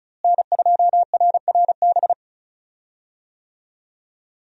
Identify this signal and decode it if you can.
Morse code, heard as a keyed tone, about 36 wpm, 700 Hz. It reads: N2RRB
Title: USA Call Signs - 5 Characters - Repeated 3x - 35wpm